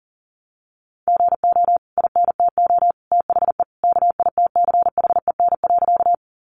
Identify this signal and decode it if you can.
Morse code, heard as a keyed tone, about 40 wpm, 700 Hz. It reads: GO INTO THE KITCHEN.
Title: Sentences from Top 1400 Words 40wpm